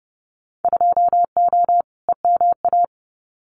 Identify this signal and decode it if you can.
Morse code, heard as a keyed tone, about 30 wpm, 700 Hz. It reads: 2O EMA
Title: Winter Field Day Exchanges 30wpm